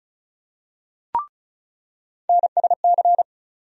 Morse code, heard as a keyed tone, about 36 wpm, 700 Hz. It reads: NSC
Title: USA Call Sign Postfixes - 3 Letters 35wpm